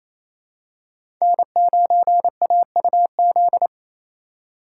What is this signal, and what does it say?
Morse code, heard as a keyed tone, about 27 wpm, 700 Hz. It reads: N9AUZ
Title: USA Call Signs - 5 Characters - 4x Character Spacing - 28wpm